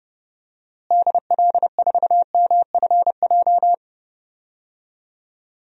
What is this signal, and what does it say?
Morse code, heard as a keyed tone, about 30 wpm, 700 Hz. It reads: DL4MFJ
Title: DXCC-100 Call Signs - 6 Characters - Repeated 3x - 30wpm